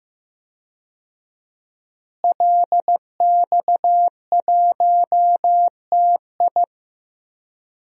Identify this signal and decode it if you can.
Morse code, heard as a keyed tone, about 15 wpm, 700 Hz. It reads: LX1TI
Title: DXCC-100 Call Signs - 5 Characters - Repeated 3x - 15wpm